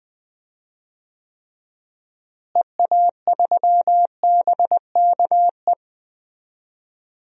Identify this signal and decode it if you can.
Morse code, heard as a keyed tone, about 20 wpm, 700 Hz. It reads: EA3BKE
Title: DXCC-100 Call Signs - 6 Characters - Repeated 3x - 20wpm